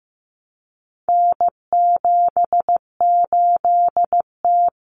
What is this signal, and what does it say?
Morse code, heard as a keyed tone, about 15 wpm, 700 Hz. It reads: N78T
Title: Common Cut Numbers - 4-Digit 15wpm